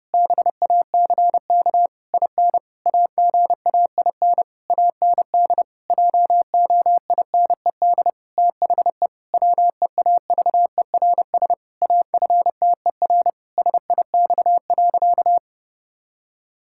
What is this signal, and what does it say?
Morse code, heard as a keyed tone, about 30 wpm, 700 Hz. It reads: BACK IN AGAIN AND JOINED THE WEAVERS AFTER SIX.
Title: Pangram Challenge v2 - 30wpm